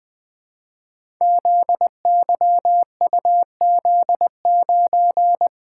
Morse code, heard as a keyed tone, about 20 wpm, 700 Hz. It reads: ZYUZ9
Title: Code Groups - 5 Letter:Numbers - 20wpm